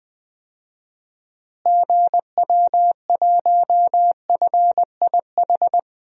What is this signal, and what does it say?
Morse code, heard as a keyed tone, about 20 wpm, 700 Hz. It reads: GW1FIH
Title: DXCC-100 Call Signs - 6 Characters - Repeated 3x - 20wpm